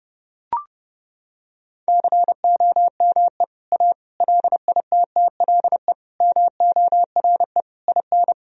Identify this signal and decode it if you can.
Morse code, heard as a keyed tone, about 30 wpm, 700 Hz. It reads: COME A LITTLE MORE IN
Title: Sentences from Top 1000 Words - Review Part 2 - 30wpm